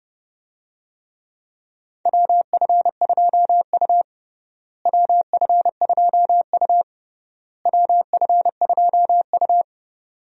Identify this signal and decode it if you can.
Morse code, heard as a keyed tone, about 29 wpm, 700 Hz. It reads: WF2U WF2U WF2U
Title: USA Call Signs - 4 Characters - Repeated 3x - 30wpm